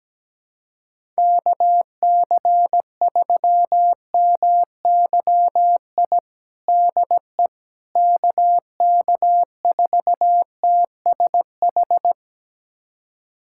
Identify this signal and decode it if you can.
Morse code, heard as a keyed tone, about 17 wpm, 700 Hz. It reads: KC3MYI DE KK4TSH
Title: USA Callsign DE Callsign - 6 Characters - 17wpm